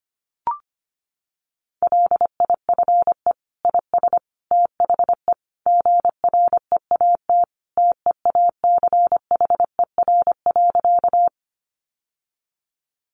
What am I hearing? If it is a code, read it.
Morse code, heard as a keyed tone, about 25 wpm, 700 Hz. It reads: LIFE IS THE GREAT TEACHER.
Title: Sentences from Top 900 Words 25wpm